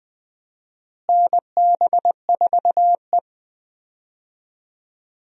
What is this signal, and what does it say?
Morse code, heard as a keyed tone, about 20 wpm, 700 Hz. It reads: NB4E
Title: USA Call Signs - 4 Characters - Repeated 2x - 20wpm